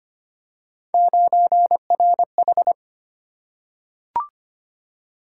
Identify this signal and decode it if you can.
Morse code, heard as a keyed tone, about 25 wpm, 700 Hz. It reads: 9RH
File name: Three Letter-Number Combinations - 25wpm